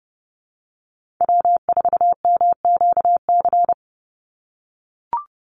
Morse code, heard as a keyed tone, about 30 wpm, 700 Hz. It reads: W4MQC
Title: Call Signs - 30wpm